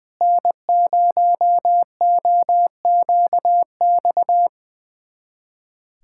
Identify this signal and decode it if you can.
Morse code, heard as a keyed tone, about 21 wpm, 700 Hz. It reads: N0OQX
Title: USA Callsign DE Callsign - 5 Characters - 20wpm